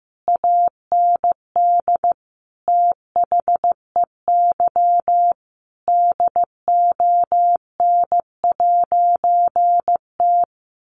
Morse code, heard as a keyed tone, about 15 wpm, 700 Hz. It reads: AND THEY DON'T
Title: A Sound Of Thunder 15wpm